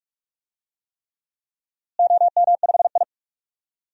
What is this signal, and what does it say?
Morse code, heard as a keyed tone, about 41 wpm, 700 Hz. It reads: OM5I
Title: DXCC-100 Call Signs - 4 Characters - Repeated 3x - 45wpm